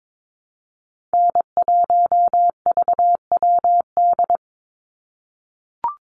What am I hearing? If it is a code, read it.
Morse code, heard as a keyed tone, about 22 wpm, 700 Hz. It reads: N1VWD
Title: NA SOTA Chaser Callsigns 22wpm